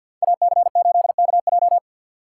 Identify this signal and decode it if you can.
Morse code, heard as a keyed tone, about 52 wpm, 700 Hz. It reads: AC8KJ
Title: USA Call Signs - 5 Characters - 3x Character Spacing - 50wpm